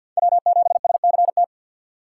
Morse code, heard as a keyed tone, about 52 wpm, 700 Hz. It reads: W7SCT
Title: USA Call Signs - 5 Characters - 2x Character Spacing - 50wpm